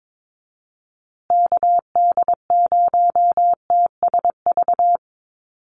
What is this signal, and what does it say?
Morse code, heard as a keyed tone, about 22 wpm, 700 Hz. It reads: KD0TSV
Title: USA Call Signs Repeated 2x 22wpm